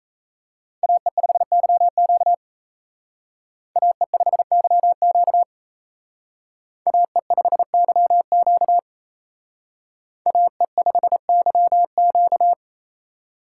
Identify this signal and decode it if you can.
Morse code, heard as a keyed tone, about 35 wpm, 700 Hz. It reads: AE5YQ AE5YQ AE5YQ AE5YQ